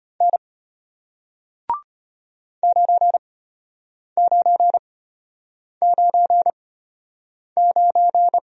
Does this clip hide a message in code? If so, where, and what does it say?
Yes: Morse code from 0.2 to 8.4 s, heard as a keyed tone, about 33 wpm, spaced as for 8 wpm, 700 Hz. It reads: N 9999